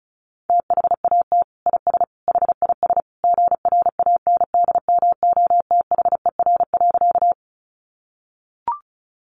Morse code, heard as a keyed tone, about 34 wpm, 700 Hz. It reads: THAT IS HIS GRANDMOTHER.
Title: Sentences from Top 1200 Words 35wpm